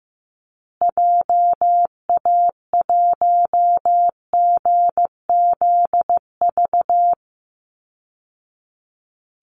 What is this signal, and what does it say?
Morse code, heard as a keyed tone, about 15 wpm, 700 Hz. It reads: JA1GZV
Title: Call Signs Repeated 2x - 15wpm